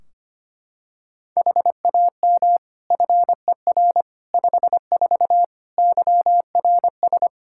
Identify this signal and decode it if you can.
Morse code, heard as a keyed tone, about 25 wpm, 700 Hz. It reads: HAM FER 54 YRS
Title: Practice QSO - Exchange 3 - More Info - 25wpm